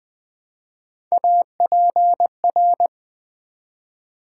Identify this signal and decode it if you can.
Morse code, heard as a keyed tone, about 20 wpm, 700 Hz. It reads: APR
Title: Build A Word - 1 Letter At A Time - 3 to 5-Letter Words - 20wpm